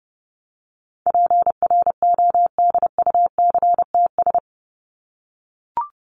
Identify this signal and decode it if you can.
Morse code, heard as a keyed tone, about 30 wpm, 700 Hz. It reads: PRODUCTS
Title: ES:S Suffix - Top 100 Words with Base Words 30wpm